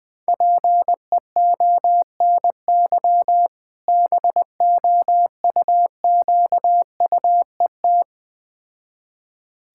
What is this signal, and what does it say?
Morse code, heard as a keyed tone, about 20 wpm, 700 Hz. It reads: PEONY BOUQUET
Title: Difficult Words with Hints - 20wpm